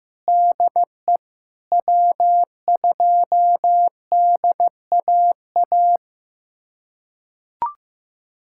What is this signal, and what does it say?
Morse code, heard as a keyed tone, about 15 wpm, 700 Hz. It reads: DE W2DAA
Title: USA Callsign DE Callsign - 5 Characters - 15wpm